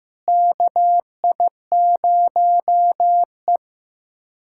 Morse code, heard as a keyed tone, about 15 wpm, 700 Hz. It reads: KI0E
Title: USA Callsign DE Callsign - 4 Characters - 15wpm